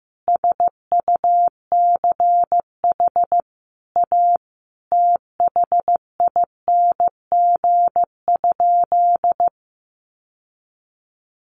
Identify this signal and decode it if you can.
Morse code, heard as a keyed tone, about 15 wpm, 700 Hz. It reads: SUCH A THING?
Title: Sentences from Top 1400 Words 15wpm